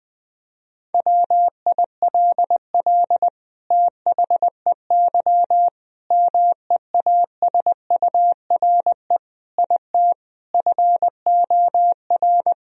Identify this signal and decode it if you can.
Morse code, heard as a keyed tone, about 21 wpm, 700 Hz. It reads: WILL THEY MEASURE IT FOR
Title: Sentences from Top 1000 Words - Review Part 2 - 20wpm